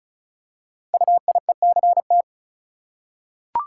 Morse code, heard as a keyed tone, about 36 wpm, 690 Hz. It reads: UIECT
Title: Code Groups - 5 Letters - 35wpm